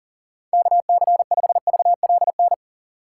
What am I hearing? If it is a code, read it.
Morse code, heard as a keyed tone, about 38 wpm, 700 Hz. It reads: KC5VLN
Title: USA Call Signs - 6 Characters - 2x Character Spacing - 40wpm